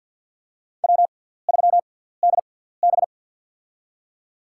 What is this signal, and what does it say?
Morse code, heard as a keyed tone, about 51 wpm, spaced as for 17 wpm, 700 Hz. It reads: W3DB
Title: USA Call Signs - 4 Characters - 6x Character Spacing - 50wpm